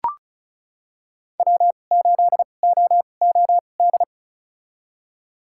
Morse code, heard as a keyed tone, about 36 wpm, spaced as for 25 wpm, 700 Hz. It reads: W8OOD
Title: USA Call Signs - 5 Characters - 2x Character Spacing - 35wpm